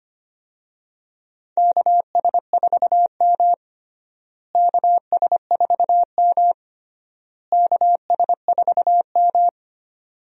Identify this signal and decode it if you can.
Morse code, heard as a keyed tone, about 25 wpm, 700 Hz. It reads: KS4M KS4M KS4M
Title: USA Call Signs - 4 Characters - Repeated 3x - 25wpm